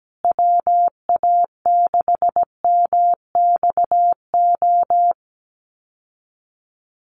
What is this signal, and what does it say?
Morse code, heard as a keyed tone, about 17 wpm, 700 Hz. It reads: WA6MXO
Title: USA Call Signs Repeated 2x 17wpm